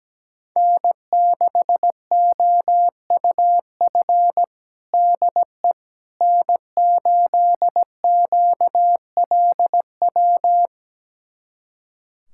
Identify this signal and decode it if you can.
Morse code, heard as a keyed tone, about 17 wpm, 700 Hz. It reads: N6OUF DE N8QLW
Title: USA Callsign DE Callsign - 5 Characters - 17wpm